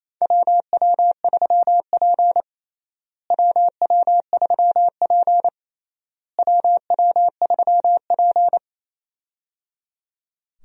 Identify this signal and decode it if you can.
Morse code, heard as a keyed tone, about 28 wpm, 700 Hz. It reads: WW3P WW3P WW3P
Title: USA Call Signs - 4 Characters - Repeated 3x - 28wpm